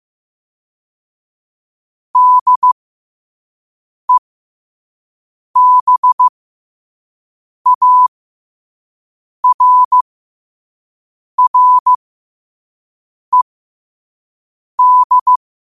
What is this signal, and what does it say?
Morse code, heard as a keyed tone, about 15 wpm, spaced as for 5 wpm, 1000 Hz. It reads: DEBARRED
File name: qotdfiles / 5wpm